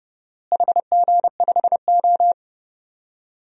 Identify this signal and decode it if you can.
Morse code, heard as a keyed tone, about 32 wpm, 700 Hz. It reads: HG5O
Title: DXCC-100 Call Signs - 4 Characters - Repeated 2x - 30wpm